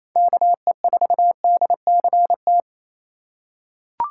Morse code, heard as a keyed tone, about 28 wpm, 700 Hz. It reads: KE4DCT
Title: USA Call Signs - 2x Character Spacing - 28wpm